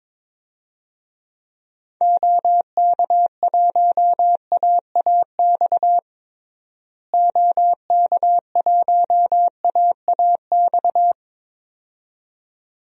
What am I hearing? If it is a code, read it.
Morse code, heard as a keyed tone, about 22 wpm, 700 Hz. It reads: OK1AAX OK1AAX
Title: DXCC-100 Call Signs - 6 Characters - Repeated 2x - 22wpm